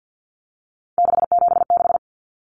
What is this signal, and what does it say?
Morse code, heard as a keyed tone, about 50 wpm, 700 Hz. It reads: 676
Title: Numbers - 3-Digit 50wpm